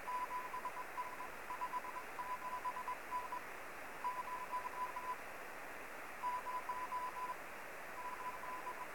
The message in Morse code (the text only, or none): none